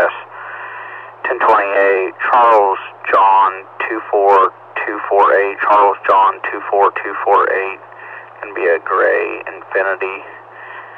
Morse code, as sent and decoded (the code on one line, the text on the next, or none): none